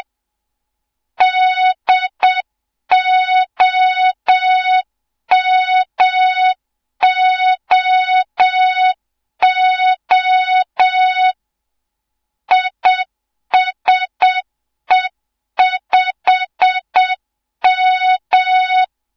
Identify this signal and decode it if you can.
Morse code, heard as a keyed tone, about 7 wpm, 3700 Hz. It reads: DOMOO ISE5M
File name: Morze-HA2MN / hang / 07 GYAKORLAT